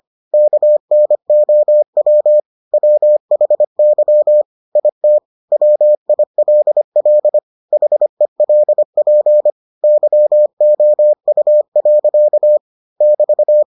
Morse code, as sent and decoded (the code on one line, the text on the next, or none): -.- -. --- .-- / .-- .... -.-- / .. - / .-- .. .-.. .-.. / .... .-.. .--. / -.-- --- ..- .-.-.- / -...-
KNOW WHY IT WILL HELP YOU. <BT>